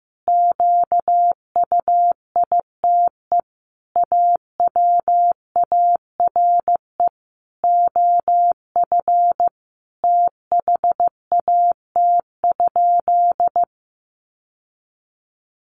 Morse code, heard as a keyed tone, about 15 wpm, 700 Hz. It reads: QUITE AWARE OF THAT?
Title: Sentences from Top 1200 Words 15wpm